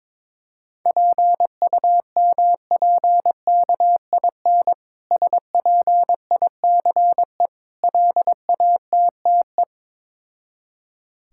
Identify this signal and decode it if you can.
Morse code, heard as a keyed tone, about 22 wpm, 690 Hz. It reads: PUMPKIN SPICE LATTE
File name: Difficult Words with Hints - 22wpm